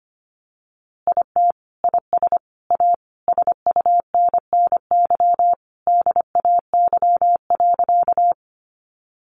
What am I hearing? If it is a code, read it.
Morse code, heard as a keyed tone, about 25 wpm, 700 Hz. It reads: IT IS A SUNNY DAY.
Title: Easy Phrases 25wpm